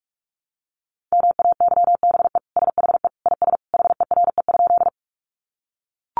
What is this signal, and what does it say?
Morse code, heard as a keyed tone, about 44 wpm, 700 Hz. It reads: MAYBE SHE IS HERE?
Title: Sentences from Top 1000 Words 45wpm